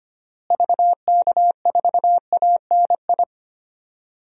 Morse code, heard as a keyed tone, about 25 wpm, 700 Hz. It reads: VK4ANI
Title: DXCC-100 Call Signs - 6 Characters - Repeated 2x - 25wpm